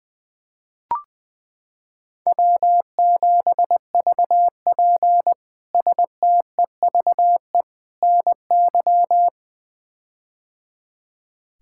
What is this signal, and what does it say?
Morse code, heard as a keyed tone, about 21 wpm, 700 Hz. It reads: W7VP STEVE NY
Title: K1USN SST QSO Practice - 20wpm